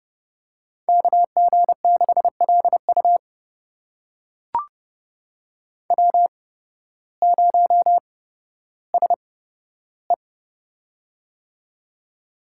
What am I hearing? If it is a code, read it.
Morse code, heard as a keyed tone, about 29 wpm, spaced as for 14 wpm, 700 Hz. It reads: KG6LU W0SE
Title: USA Call Signs - 8x Character Spacing - 30wpm